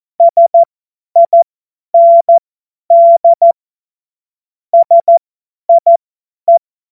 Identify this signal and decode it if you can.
Morse code, heard as a keyed tone, about 14 wpm, spaced as for 10 wpm, 680 Hz. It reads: SIND SIE
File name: Morsekurs / Witze / Witz 07.November 2023 50 F70